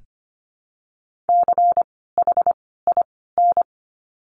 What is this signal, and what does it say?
Morse code, heard as a keyed tone, about 25 wpm, spaced as for 16 wpm, 700 Hz. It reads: CHIN
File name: course / 25:10wpm / 094 Lesson 16f - Letter c - Words 25wpm